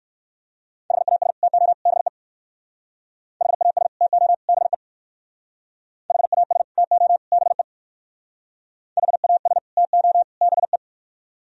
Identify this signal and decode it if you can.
Morse code, heard as a keyed tone, about 58 wpm, 700 Hz. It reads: HAS TO BE HAS TO BE HAS TO BE HAS TO BE